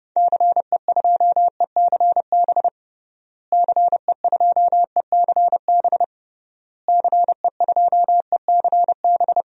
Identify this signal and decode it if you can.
Morse code, heard as a keyed tone, about 32 wpm, 700 Hz. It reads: CE2ECB CE2ECB CE2ECB
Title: DXCC-100 Call Signs - 6 Characters - Repeated 3x - 30wpm